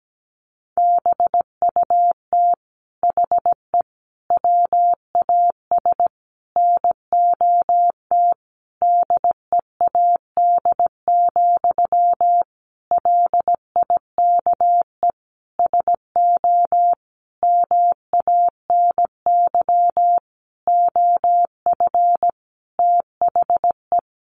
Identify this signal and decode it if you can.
Morse code, heard as a keyed tone, about 17 wpm, 700 Hz. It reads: BUT HE WAS NOT DEAD, LIKE SO MANY OF THE
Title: Sentences from Top 1500 Words 17wpm